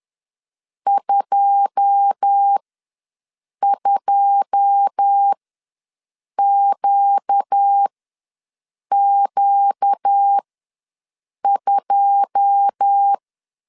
Morse code, heard as a keyed tone, about 10 wpm, spaced as for 6 wpm, 800 Hz. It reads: 22QQ2
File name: LEZIONI / 9 2 ws 10wpm 4130